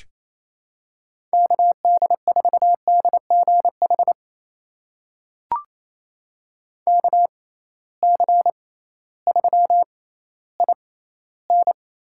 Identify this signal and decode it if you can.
Morse code, heard as a keyed tone, about 28 wpm, spaced as for 15 wpm, 700 Hz. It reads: KD4DGH KC3IN
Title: USA Call Signs - 6 Characters - 6x Character Spacing - 28wpm